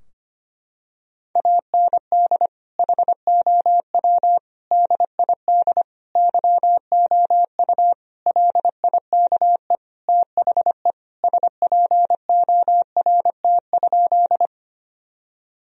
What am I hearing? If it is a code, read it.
Morse code, heard as a keyed tone, about 25 wpm, 700 Hz. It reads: AND HOW DID YOU LIKE THE SPORT?
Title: Sentences from Top 1000 Words - Review Part 4 - 25wpm